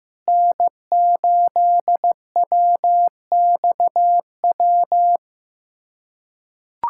Morse code, heard as a keyed tone, about 15 wpm, 700 Hz. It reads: N8WXW
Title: USA Callsign DE Callsign - 5 Characters - 15wpm